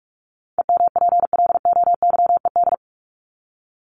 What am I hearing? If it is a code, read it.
Morse code, heard as a keyed tone, about 44 wpm, 700 Hz. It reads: EMPLOYED
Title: ED Suffix - Top 100 Words with Base Words 45wpm